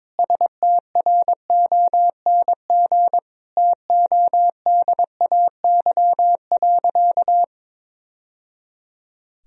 Morse code, heard as a keyed tone, about 22 wpm, 700 Hz. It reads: STRONG TODAY.
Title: Sentences from Top 1000 Words - Review Part 2 - 22wpm